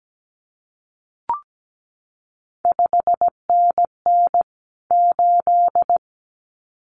Morse code, heard as a keyed tone, about 17 wpm, 700 Hz. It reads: 5NN 8
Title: CQ WW DX Contest Exchange - 17wpm